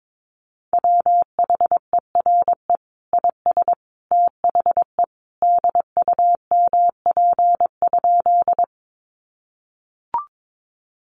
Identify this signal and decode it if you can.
Morse code, heard as a keyed tone, about 22 wpm, 700 Hz. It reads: WHERE IS THE DUMP?
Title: Sentences from Top 1300 Words 22wpm